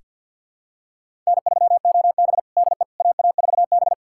Morse code, heard as a keyed tone, about 47 wpm, 700 Hz. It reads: N3OB DE AA4B
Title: USA Callsign DE Callsign - 4 Characters - 50wpm